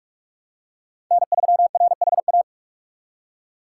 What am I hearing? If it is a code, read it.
Morse code, heard as a keyed tone, about 41 wpm, 700 Hz. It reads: N3RHA